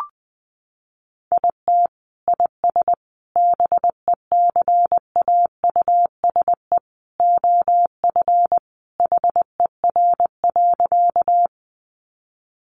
Morse code, heard as a keyed tone, about 20 wpm, 700 Hz. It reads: IT IS BECAUSE OF HER.
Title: Sentences from Top 600 Words 20wpm